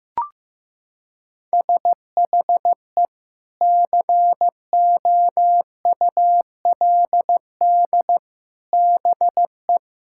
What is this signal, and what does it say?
Morse code, heard as a keyed tone, about 16 wpm, 700 Hz. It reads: SHE COULD BE
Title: Sentences from Top 1600 Words 15wpm